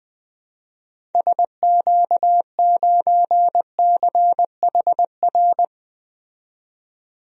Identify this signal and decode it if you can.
Morse code, heard as a keyed tone, about 20 wpm, 700 Hz. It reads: SQ9CHR
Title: DXCC-100 Call Signs - 6 Characters - 20wpm